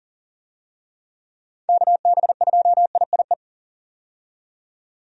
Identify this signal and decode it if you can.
Morse code, heard as a keyed tone, about 40 wpm, 700 Hz. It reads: KB2IIE
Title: USA Call Signs - 6 Characters - 40wpm